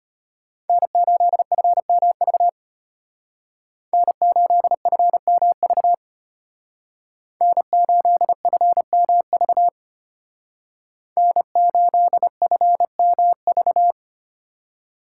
Morse code, heard as a keyed tone, about 33 wpm, 700 Hz. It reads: N8FMV N8FMV N8FMV N8FMV